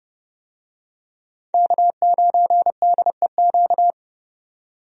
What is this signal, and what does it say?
Morse code, heard as a keyed tone, about 30 wpm, 700 Hz. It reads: K9DEQ
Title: USA Call Signs - 5 Characters - 30wpm